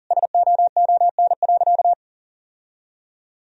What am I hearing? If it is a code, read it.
Morse code, heard as a keyed tone, about 40 wpm, 700 Hz. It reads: SOON.
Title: Sentences from Top 1000 Words - Review Part 1 - 40wpm